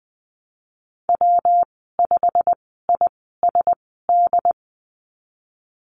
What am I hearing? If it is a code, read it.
Morse code, heard as a keyed tone, about 20 wpm, spaced as for 15 wpm, 700 Hz. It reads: W5ISD
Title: 186 Lesson 30g - Letter b - Call Signs 20wpm